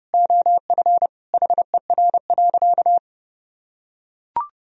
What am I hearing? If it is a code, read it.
Morse code, heard as a keyed tone, about 32 wpm, 700 Hz. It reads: OF HER.
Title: Sentences from Top 1000 Words - Review Part 2 - 30wpm